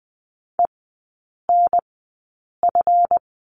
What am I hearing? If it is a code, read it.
Morse code, heard as a keyed tone, about 20 wpm, spaced as for 8 wpm, 700 Hz. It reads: ENF
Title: 191 Lesson 31e - Letter v - Three Characters 20wpm